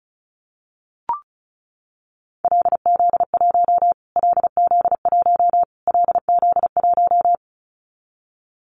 Morse code, heard as a keyed tone, about 34 wpm, 700 Hz. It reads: LZ1 LZ1 LZ1
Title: Callsign Prefixes for Top 25 DXCC - Repeated 3x - 35wpm